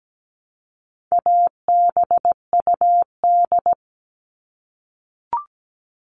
Morse code, heard as a keyed tone, about 17 wpm, 700 Hz. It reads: ABUD
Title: All Cut Numbers - 4-Digit 17wpm